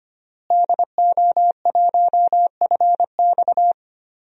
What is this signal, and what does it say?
Morse code, heard as a keyed tone, about 25 wpm, 700 Hz. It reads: DO1FX
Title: DXCC-100 Call Signs - 5 Characters - Repeated 3x - 25wpm